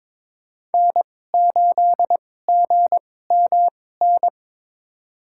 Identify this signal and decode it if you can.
Morse code, heard as a keyed tone, about 22 wpm, spaced as for 16 wpm, 700 Hz. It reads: N8GMN
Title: USA Call Signs - 5 Characters - 2x Character Spacing - 22wpm